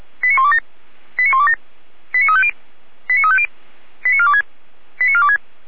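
Five-tone selective call (ZVEI), heard as quick runs of five tones, tones of about 80 ms. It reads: 89127 89127 89350 89350 89426 89426
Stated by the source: ZVEI five-tone selective call received by SDR, archived by the Orlamünde fire brigade (feuerwehr-orlamuende).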